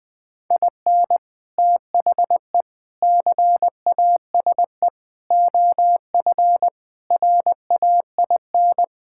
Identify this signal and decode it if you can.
Morse code, heard as a keyed tone, about 21 wpm, 700 Hz. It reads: IN THE CASE OF RAIN
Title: Sentences with 3-Word Phrases - Spoken Beginning - 20wpm